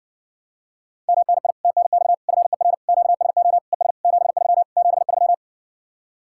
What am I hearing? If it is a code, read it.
Morse code, heard as a keyed tone, about 60 wpm, 700 Hz. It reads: MNI TNX FER QSO ES 73 73